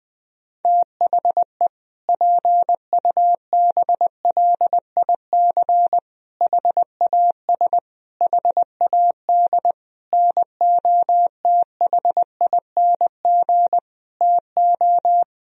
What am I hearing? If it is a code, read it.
Morse code, heard as a keyed tone, about 21 wpm, 700 Hz. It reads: THE PUBLIC HAS HAD NOTHING TO
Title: Sentences from Top 1000 Words - Review Part 3 - 20wpm